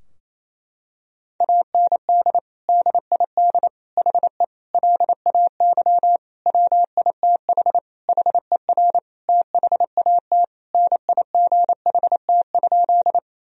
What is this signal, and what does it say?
Morse code, heard as a keyed tone, about 27 wpm, 700 Hz. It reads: AND DID HE LAY WITH HER THAT NIGHT?
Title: Sentences from Top 1000 Words - Review Part 2 - 28wpm